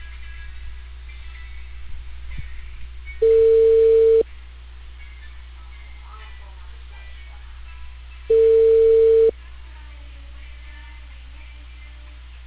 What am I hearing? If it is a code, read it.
Ringback tone (China, 450 Hz).